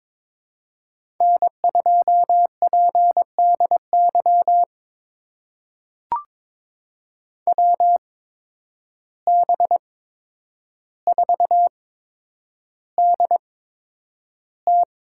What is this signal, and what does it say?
Morse code, heard as a keyed tone, about 22 wpm, spaced as for 9 wpm, 700 Hz. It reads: N2PDY WB4DT
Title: USA Call Signs - 5 Characters - 8x Character Spacing - 22wpm